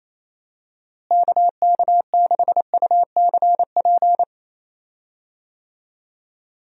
Morse code, heard as a keyed tone, about 28 wpm, 700 Hz. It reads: KK6UCP
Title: USA Call Signs - 6 Characters - Repeated 2x - 28wpm